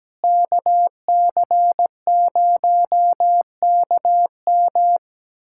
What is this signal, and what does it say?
Morse code, heard as a keyed tone, about 17 wpm, 700 Hz. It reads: KC0KM
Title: USA Callsign DE Callsign - 5 Characters - 17wpm